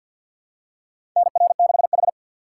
Morse code, heard as a keyed tone, about 47 wpm, 700 Hz. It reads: NR6H
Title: USA Call Signs - 4 Characters - 6x Character Spacing - 50wpm